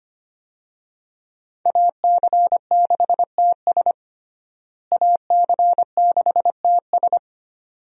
Morse code, heard as a keyed tone, about 25 wpm, 700 Hz. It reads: AC6TS AC6TS
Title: USA Call Signs - 5 Characters - Repeated 2x - 25wpm